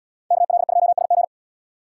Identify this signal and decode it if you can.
Morse code, heard as a keyed tone, about 68 wpm, 700 Hz. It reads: KL2SG